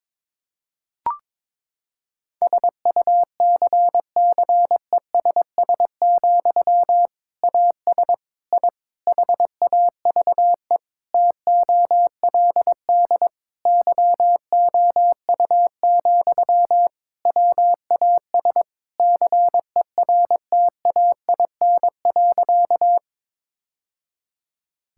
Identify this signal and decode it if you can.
Morse code, heard as a keyed tone, about 22 wpm, 700 Hz. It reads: SUCCESS, AS I HAVE TOLD YOU, WAS CERTAIN.
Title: Sentences from Top 1000 Words - Review Part 6 - 22wpm